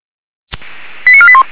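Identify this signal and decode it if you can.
Five-tone selective call (ZVEI), heard as quick runs of five tones, tones of about 70 ms. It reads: 89481
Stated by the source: ZVEI five-tone selective call received by SDR on channel 464, recorded for feuerwehr-orlamuende.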